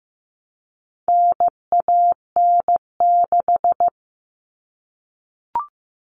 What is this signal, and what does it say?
Morse code, heard as a keyed tone, about 15 wpm, 700 Hz. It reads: NAN6
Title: All Cut Numbers - 4-Digit 15wpm